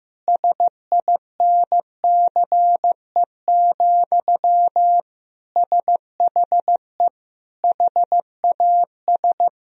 Morse code, heard as a keyed tone, about 16 wpm, 700 Hz. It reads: SINCE, SHE HAS
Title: Sentences from Top 1000 Words - Review Part 1 - 15wpm